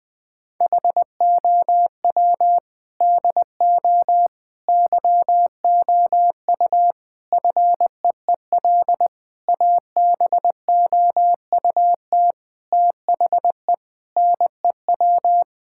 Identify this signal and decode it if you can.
Morse code, heard as a keyed tone, about 21 wpm, 700 Hz. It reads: HOW DO YOU FEEL ABOUT THE NEW
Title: Sentences with 5-Word Phrases - Spoken Beginning - 20wpm